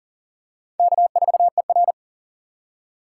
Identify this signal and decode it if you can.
Morse code, heard as a keyed tone, about 43 wpm, 690 Hz. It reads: K4ER
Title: USA Call Signs - 4 Characters - 2x Character Spacing - 40wpm